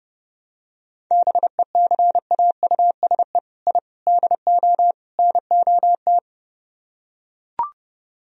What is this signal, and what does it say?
Morse code, heard as a keyed tone, about 29 wpm, 700 Hz. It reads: BECAUSE I DO NOT